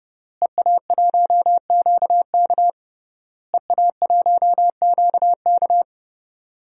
Morse code, heard as a keyed tone, about 30 wpm, 700 Hz. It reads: EA1QK EA1QK
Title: DXCC-100 Call Signs - 5 Characters - Repeated 3x - 30wpm